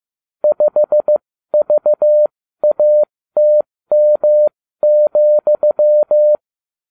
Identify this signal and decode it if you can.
Morse code, heard as a keyed tone, about 15 wpm, spaced as for 13 wpm, 600 Hz. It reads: 5VATM,
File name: cw / ubacw29 2